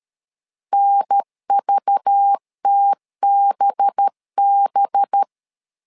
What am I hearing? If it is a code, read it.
Morse code, heard as a keyed tone, about 13 wpm, 800 Hz. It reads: NVTBB
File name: LEZIONI / 5 LBVC / 5 LBV 12wpm 1855